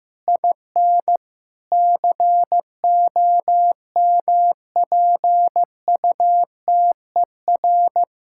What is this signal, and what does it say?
Morse code, heard as a keyed tone, about 15 wpm, 700 Hz. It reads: IN COMPUTER
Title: Sentences from Top 1000 Words - Review Part 6 - 15wpm